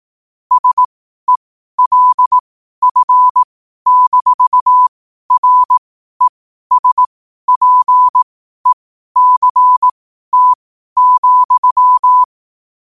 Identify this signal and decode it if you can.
Morse code, heard as a keyed tone, about 18 wpm, spaced as for 12 wpm, 1000 Hz. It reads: SELF-RESPECT,
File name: qotdfiles / 12wpm